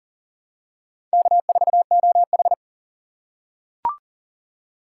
Morse code, heard as a keyed tone, about 40 wpm, 700 Hz. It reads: K4OH
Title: USA Call Signs - 4 Characters - 2x Character Spacing - 40wpm